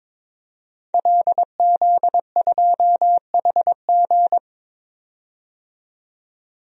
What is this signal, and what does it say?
Morse code, heard as a keyed tone, about 22 wpm, 700 Hz. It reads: LZ2HG
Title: DXCC-100 Call Signs - 5 Characters - 22wpm